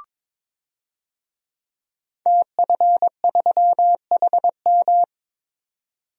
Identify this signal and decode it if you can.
Morse code, heard as a keyed tone, about 22 wpm, 700 Hz. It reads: TF3HM
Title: DXCC-100 Call Signs - 5 Characters - 22wpm